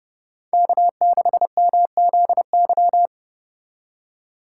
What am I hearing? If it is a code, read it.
Morse code, heard as a keyed tone, about 29 wpm, 700 Hz. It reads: K6MZY